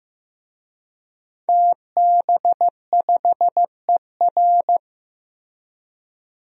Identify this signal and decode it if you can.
Morse code, heard as a keyed tone, about 15 wpm, 700 Hz. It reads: TB5ER
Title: DXCC-100 Call Signs - 5 Characters - Repeated 2x - 15wpm